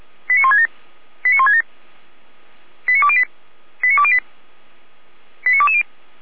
Five-tone selective call (ZVEI), heard as quick runs of five tones, tones of about 70 ms. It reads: 89167 89167 89208 89208 89220
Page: /455/SDR/ZVEI/89097/